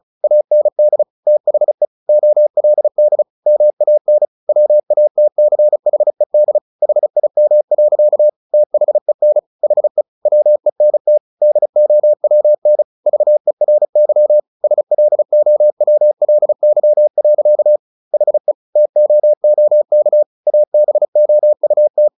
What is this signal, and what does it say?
Morse code, heard as a keyed tone, about 35 wpm, 600 Hz. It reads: AND THE OLD MAN WATCHED HIM. THEN HE WENT DOWN VERY SLOWLY. HE TOOK ABOUT
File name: the old man and the sea 2 35wpm